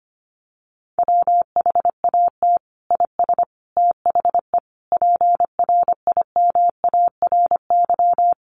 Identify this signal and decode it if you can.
Morse code, heard as a keyed tone, about 25 wpm, 690 Hz. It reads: WHAT IS THE PRIMARY
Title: Sentences from Top 1100 Words 25wpm